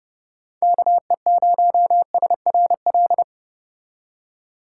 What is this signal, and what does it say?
Morse code, heard as a keyed tone, about 30 wpm, 700 Hz. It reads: KE0SRL
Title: K1USN SST QSO Practice - 30wpm